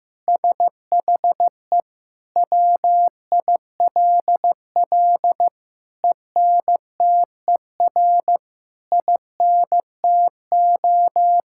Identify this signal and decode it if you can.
Morse code, heard as a keyed tone, about 16 wpm, 700 Hz. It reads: SHE WILL ENTER INTO
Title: Sentences from Top 1000 Words - Review Part 6 - 15wpm